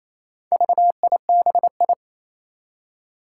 Morse code, heard as a keyed tone, about 27 wpm, 700 Hz. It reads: VIBI
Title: Four Challenging Letter-Number Combinations 28wpm